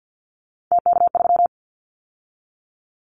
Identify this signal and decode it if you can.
Morse code, heard as a keyed tone, about 50 wpm, 700 Hz. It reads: TK3
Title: Callsign Prefixes for Top 100 DXCC - 50wpm